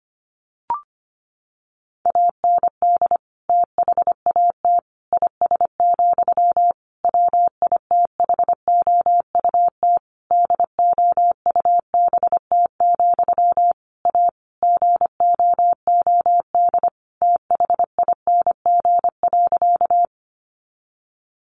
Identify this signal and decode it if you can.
Morse code, heard as a keyed tone, about 25 wpm, 700 Hz. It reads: AND THAT IS, WITHOUT DOUBT, A GOOD THING.
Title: Sentences from Top 1500 Words 25wpm